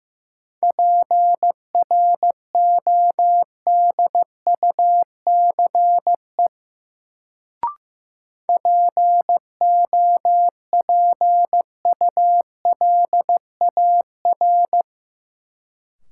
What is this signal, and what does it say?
Morse code, heard as a keyed tone, about 15 wpm, 700 Hz. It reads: PRODUCE POPULAR
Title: Sentences from Top 1000 Words - Review Part 3 - New Word Drill - 15wpm